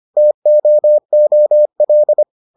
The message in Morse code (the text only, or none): TOOL